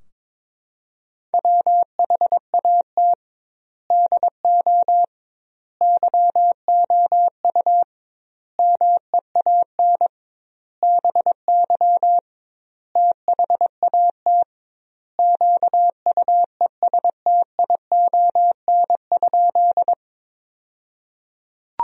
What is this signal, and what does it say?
Morse code, heard as a keyed tone, about 22 wpm, 700 Hz. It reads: WHAT DO YOU MEAN BY THAT QUESTION?
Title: 2x Word-Spacing - Sentences from Top 400 Words - 22wpm